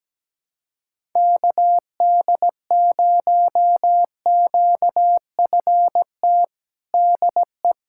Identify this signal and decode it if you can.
Morse code, heard as a keyed tone, about 17 wpm, 700 Hz. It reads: KD0QFT DE
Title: USA Callsign DE Callsign - 6 Characters - 17wpm